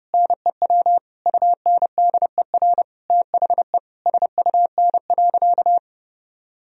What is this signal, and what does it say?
Morse code, heard as a keyed tone, about 32 wpm, 700 Hz. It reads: NEW UNDER THE SUN.
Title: Sentences from Top 1000 Words - Review Part 1 - 30wpm